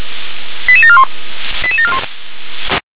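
Five-tone selective call (ZVEI), heard as quick runs of five tones, tones of about 70 ms. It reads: 88741 88741
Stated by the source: ZVEI five-tone selective call received on an SDR.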